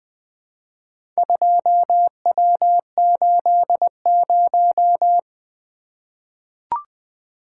Morse code, heard as a keyed tone, about 20 wpm, 700 Hz. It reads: 2W80
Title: Four Letter-Number Combinations - 20wpm